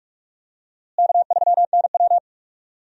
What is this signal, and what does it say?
Morse code, heard as a keyed tone, about 46 wpm, 700 Hz. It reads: K3NW
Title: USA Call Signs - 4 Characters - Repeated 2x - 45wpm